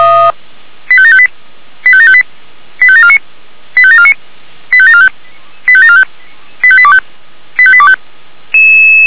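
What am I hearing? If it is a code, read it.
Five-tone selective call (ZVEI), heard as quick runs of five tones, tones of about 70 ms. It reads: 85758 85758 85749 85749 85745 85745 85825 85825
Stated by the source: ZVEI five-tone call 85827 received by SDR.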